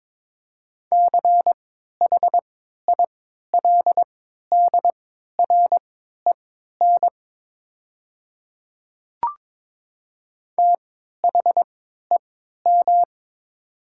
Morse code, heard as a keyed tone, about 22 wpm, spaced as for 12 wpm, 700 Hz. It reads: CHILDREN THEM
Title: 3x Character Spacing - Top 300 Words - 22wpm